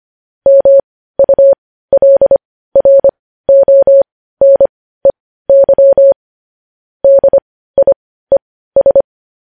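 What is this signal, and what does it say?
Morse code, heard as a keyed tone, about 25 wpm, spaced as for 15 wpm, 550 Hz. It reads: MULRONEY DIES
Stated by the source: Morse code sending of news headlines at 15 wpm.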